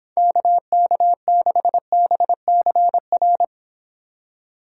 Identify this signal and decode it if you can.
Morse code, heard as a keyed tone, about 26 wpm, 700 Hz. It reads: KK6BCR